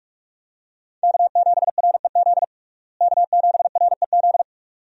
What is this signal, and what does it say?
Morse code, heard as a keyed tone, about 46 wpm, 700 Hz. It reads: K7REZ K7REZ
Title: USA Call Signs - 5 Characters - Repeated 2x - 45wpm